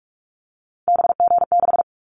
Morse code, heard as a keyed tone, about 44 wpm, 700 Hz. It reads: BG6
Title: Callsign Prefixes for Top 100 DXCC - 45wpm